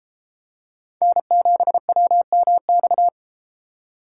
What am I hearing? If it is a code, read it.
Morse code, heard as a keyed tone, about 29 wpm, 700 Hz. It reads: N7WMX